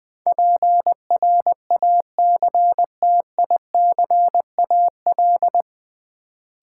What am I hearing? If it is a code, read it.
Morse code, heard as a keyed tone, about 20 wpm, 700 Hz. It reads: PRACTICAL
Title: Sentences from Top 1400 Words - New Word Drill Alt - 20wpm